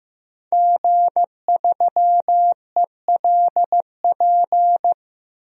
Morse code, heard as a keyed tone, about 15 wpm, 700 Hz. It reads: G3ELP
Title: DXCC-100 Call Signs - 5 Characters - Repeated 3x - 15wpm